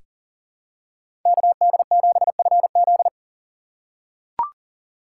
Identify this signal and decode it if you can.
Morse code, heard as a keyed tone, about 38 wpm, 700 Hz. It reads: KD7FZ
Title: USA Call Signs - 3x Character Spacing - 40wpm